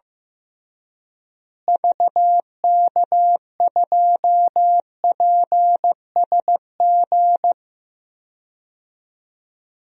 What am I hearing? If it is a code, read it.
Morse code, heard as a keyed tone, about 15 wpm, 700 Hz. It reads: VK2PSG
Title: DXCC-100 Call Signs - 6 Characters - Repeated 2x - 15wpm